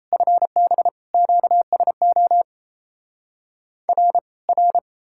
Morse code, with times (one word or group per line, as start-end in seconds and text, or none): 0.1-0.9 FB
1.1-2.4 QSO
3.9-4.2 R
4.5-4.8 R